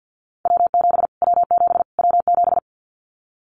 Morse code, heard as a keyed tone, about 49 wpm, 700 Hz. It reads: W7 W7 W7
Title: Callsign Prefixes for Top 25 DXCC - Repeated 3x - 50wpm